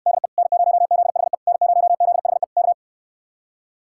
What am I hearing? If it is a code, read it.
Morse code, heard as a keyed tone, about 68 wpm, 700 Hz. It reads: DE N0ZHE N0ZHE K